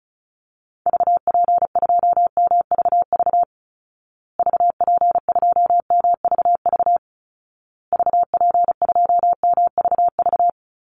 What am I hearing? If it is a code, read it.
Morse code, heard as a keyed tone, about 34 wpm, 700 Hz. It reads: VP2MVV VP2MVV VP2MVV
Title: Call Signs Repeated 2x - 35wpm